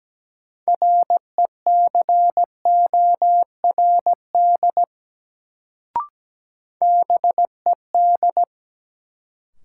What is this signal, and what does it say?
Morse code, heard as a keyed tone, about 17 wpm, 700 Hz. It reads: RECORD BED
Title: Sentences from Top 500 Words - New Word Drill - 17wpm